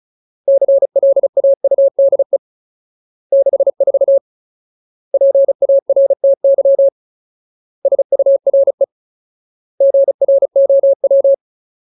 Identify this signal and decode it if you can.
Morse code, heard as a keyed tone, about 36 wpm, 550 Hz. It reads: CLAUDE 64 PARTY SURE GROW
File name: Ultimate Word List / 35WPM Ultimate